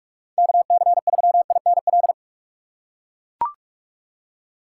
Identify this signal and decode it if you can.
Morse code, heard as a keyed tone, about 43 wpm, 700 Hz. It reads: KC3INL
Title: USA Call Signs - 6 Characters - 2x Character Spacing - 45wpm